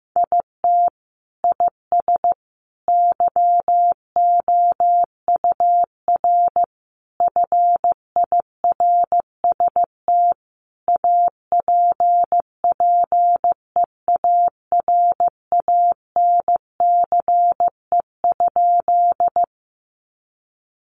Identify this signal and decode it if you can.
Morse code, heard as a keyed tone, about 15 wpm, 700 Hz. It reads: IT IS YOUR FIRST APPEARANCE?
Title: Sentences from Top 1100 Words 15wpm